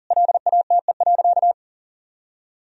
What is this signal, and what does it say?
Morse code, heard as a keyed tone, about 40 wpm, 700 Hz. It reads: LATE.
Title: Sentences from Top 1000 Words - Review Part 2 - 40wpm